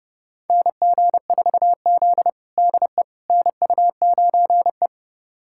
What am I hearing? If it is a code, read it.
Morse code, heard as a keyed tone, about 30 wpm, 700 Hz. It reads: NG4Z DE NU9E
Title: USA Callsign DE Callsign - 4 Characters - 30wpm